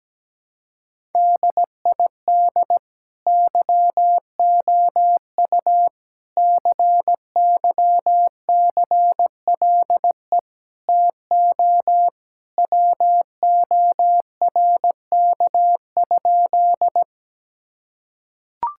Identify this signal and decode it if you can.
Morse code, heard as a keyed tone, about 17 wpm, 700 Hz. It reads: DID YOU CYCLE TO WORK?
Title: Sentences from Top 1000 Words - Review Part 6 - 17wpm